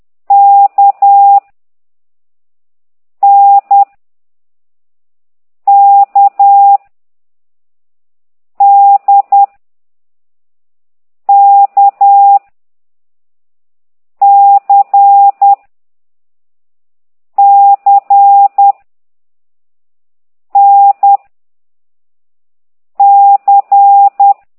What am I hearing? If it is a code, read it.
Morse code, heard as a keyed tone, about 10 wpm, 800 Hz. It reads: K N K D K C C N C
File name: exerc7